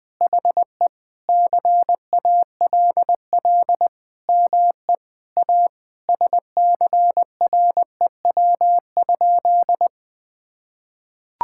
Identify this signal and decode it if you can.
Morse code, heard as a keyed tone, about 20 wpm, 700 Hz. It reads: HE CALL ME A SCREW?
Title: Sentences from Top 1600 Words 20wpm